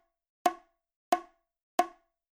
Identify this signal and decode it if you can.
There is no code.